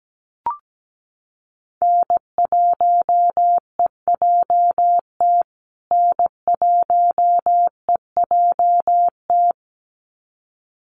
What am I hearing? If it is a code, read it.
Morse code, heard as a keyed tone, about 17 wpm, 700 Hz. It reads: N1EJT N1EJT
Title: USA Call Signs Repeated 2x 17wpm